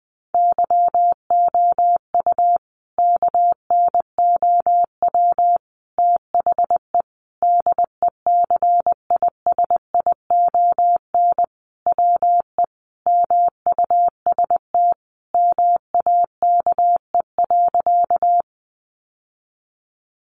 Morse code, heard as a keyed tone, about 20 wpm, 700 Hz. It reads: YOU KNOW THE DECISION WE MUST MAKE.
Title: Sentences from Top 1000 Words 20wpm